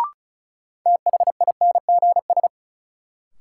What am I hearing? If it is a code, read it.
Morse code, heard as a keyed tone, about 36 wpm, 700 Hz. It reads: THINGS
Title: Sentences from Top 1400 Words - New Word Drill Alt - 35wpm